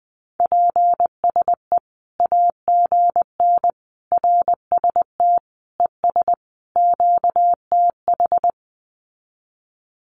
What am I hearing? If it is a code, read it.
Morse code, heard as a keyed tone, about 20 wpm, 700 Hz. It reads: PSE AGN RST ES QTH
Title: Common QSO Phrases 20wpm